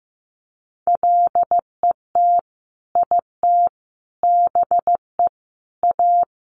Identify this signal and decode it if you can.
Morse code, heard as a keyed tone, about 15 wpm, 700 Hz. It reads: LET IT BE A
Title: Sentences from Top 1100 Words 15wpm